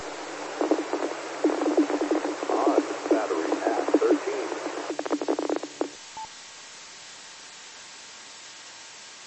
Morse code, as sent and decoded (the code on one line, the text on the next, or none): none